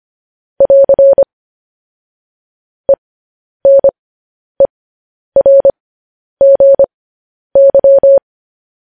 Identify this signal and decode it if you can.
Morse code, heard as a keyed tone, about 25 wpm, spaced as for 10 wpm, 550 Hz. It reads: <AR> ENERGY
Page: News Headlines in Morse Code at 10 WPM